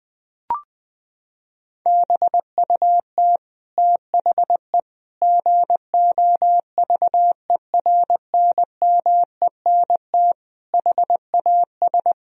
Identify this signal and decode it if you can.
Morse code, heard as a keyed tone, about 21 wpm, 700 Hz. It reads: BUT THE GOVERNMENT HAS
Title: Sentences from Top 1000 Words - Review Part 3 - 20wpm